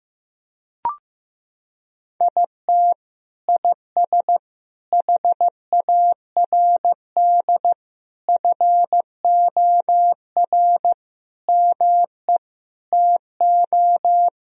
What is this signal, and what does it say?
Morse code, heard as a keyed tone, about 16 wpm, 700 Hz. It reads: IT IS HARD FOR ME TO
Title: Sentences from Top 1000 Words - Review Part 3 - 15wpm